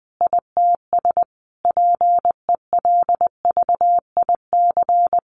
Morse code, heard as a keyed tone, about 20 wpm, 700 Hz. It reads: ITS PELVIC
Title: A Sound Of Thunder 20wpm